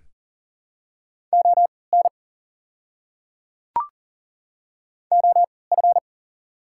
Morse code, heard as a keyed tone, about 41 wpm, spaced as for 23 wpm, 690 Hz. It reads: ON OF
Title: 3x Character Spacing - Top 200 Words - 40wpm